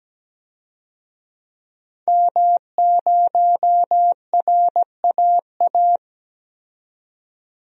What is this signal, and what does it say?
Morse code, heard as a keyed tone, about 17 wpm, 700 Hz. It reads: M0RAA
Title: DXCC-100 Call Signs - 5 Characters - 17wpm